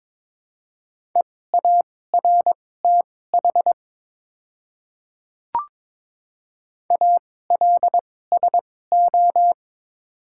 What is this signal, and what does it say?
Morse code, heard as a keyed tone, about 22 wpm, spaced as for 16 wpm, 700 Hz. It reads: EARTH ALSO
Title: 2x Character Spacing - Top 300 Words - 22wpm